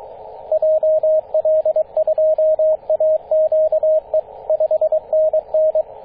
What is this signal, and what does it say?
Morse code, heard as a keyed tone, about 22 wpm, 630 Hz, with radio noise. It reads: JL2AQE 5NN
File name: 9m0o-24-891mhz-cw